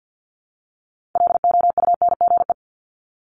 Morse code, heard as a keyed tone, about 50 wpm, 690 Hz. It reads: LOUNGE
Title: Multi-letter Phonemes Repeated 3x 50wpm